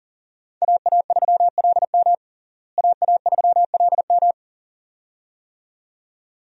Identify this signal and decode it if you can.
Morse code, heard as a keyed tone, about 40 wpm, 700 Hz. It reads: AA3LM AA3LM
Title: USA Call Signs - 5 Characters - Repeated 2x - 40wpm